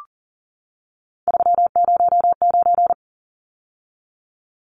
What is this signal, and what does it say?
Morse code, heard as a keyed tone, about 40 wpm, 700 Hz. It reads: 309
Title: Numbers - 3-Digit 40wpm